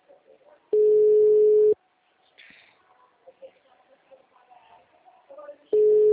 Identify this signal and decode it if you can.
Ringback tone (Europe, 425 Hz), with speech.
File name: out- -2003- - - .43128